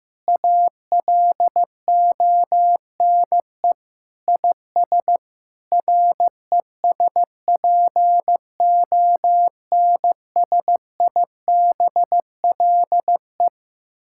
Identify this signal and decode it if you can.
Morse code, heard as a keyed tone, about 15 wpm, 700 Hz. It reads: ALONE IS RESPONSIBLE
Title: Sentences from Top 1600 Words 15wpm